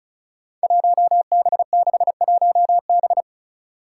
Morse code, heard as a keyed tone, about 34 wpm, 700 Hz. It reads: 1B61B
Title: Five Challenging Letter-Number Combinations 35wpm